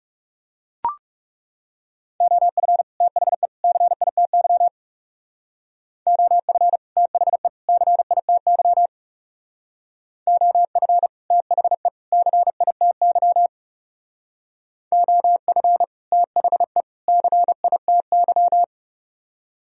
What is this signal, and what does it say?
Morse code, heard as a keyed tone, about 39 wpm, 700 Hz. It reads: OF THE CITY OF THE CITY OF THE CITY OF THE CITY